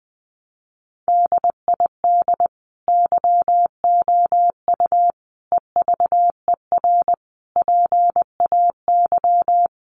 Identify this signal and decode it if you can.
Morse code, heard as a keyed tone, about 20 wpm, 700 Hz. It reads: DID YOU EVER PAY
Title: Sentences from Top 700 Words 20wpm